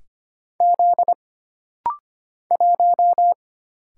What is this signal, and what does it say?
Morse code, heard as a keyed tone, about 25 wpm, 700 Hz. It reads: Z 1
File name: Single Character (Letter:Number:Punctuation:Prosign) - Rapid-Fire - 25wpm